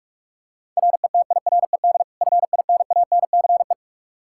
Morse code, heard as a keyed tone, about 48 wpm, 700 Hz. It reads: RETIRED FINANCE
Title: K1USN SST QSO Practice 45wpm